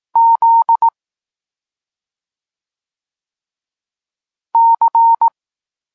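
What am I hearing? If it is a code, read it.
Morse code, heard as a keyed tone, about 18 wpm, 920 Hz. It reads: Z C
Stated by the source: Polish Morse lesson teaching the letter C, from SP9MOA Niepołomicki Klub Krótkofalowców.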